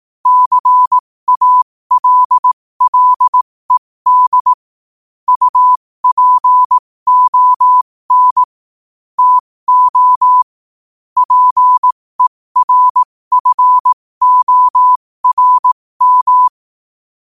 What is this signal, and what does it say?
Morse code, heard as a keyed tone, about 18 wpm, spaced as for 15 wpm, 1000 Hz. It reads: CALLED UPON TO PERFORM
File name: qotdfiles / 15wpm